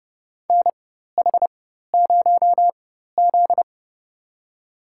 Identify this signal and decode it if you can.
Morse code, heard as a keyed tone, about 30 wpm, spaced as for 14 wpm, 700 Hz. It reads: NH0Z
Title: USA Call Signs - 4 Characters - 4x Character Spacing - 30wpm